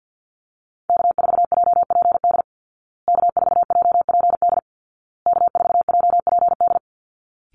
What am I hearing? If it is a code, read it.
Morse code, heard as a keyed tone, about 50 wpm, 700 Hz. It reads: K4JPD K4JPD K4JPD
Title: Call Signs Repeated 2x - 50wpm